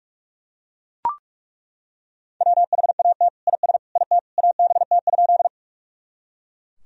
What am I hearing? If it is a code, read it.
Morse code, heard as a keyed tone, about 41 wpm, 700 Hz. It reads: WHAT IS IT ABT?
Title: Sentences from Top 100 Words and Abbreviations 45wpm